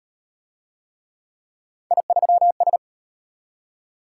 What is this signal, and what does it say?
Morse code, heard as a keyed tone, about 33 wpm, 700 Hz. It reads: I3S